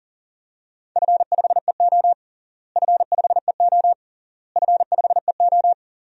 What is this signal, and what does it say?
Morse code, heard as a keyed tone, about 37 wpm, 700 Hz. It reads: F5EO F5EO F5EO
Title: DXCC-100 Call Signs - 4 Characters - Repeated 3x - 40wpm